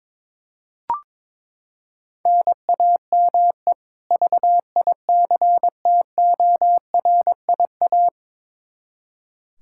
Morse code, heard as a keyed tone, about 22 wpm, 700 Hz. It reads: NAME VICTORIA
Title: Practice QSO - Exchange 1 - Beginning - 22wpm